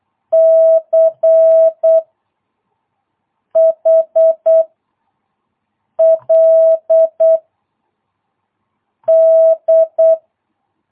Morse code, heard as a keyed tone, about 8 wpm, 660 Hz. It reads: C H L D